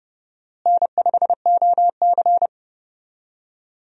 Morse code, heard as a keyed tone, about 30 wpm, 700 Hz. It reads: N5OC
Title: USA Call Signs - 4 Characters - Repeated 2x - 30wpm